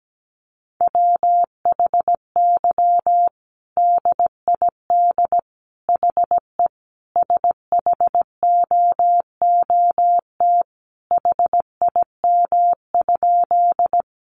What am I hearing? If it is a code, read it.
Morse code, heard as a keyed tone, about 17 wpm, 700 Hz. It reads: WHY DID HE SHOOT HIM?
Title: Sentences from Top 1200 Words 17wpm